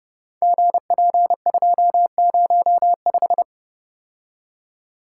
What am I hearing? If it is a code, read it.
Morse code, heard as a keyed tone, about 30 wpm, 700 Hz. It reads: GP205
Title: Code Groups - 5 Letter:Numbers - 30wpm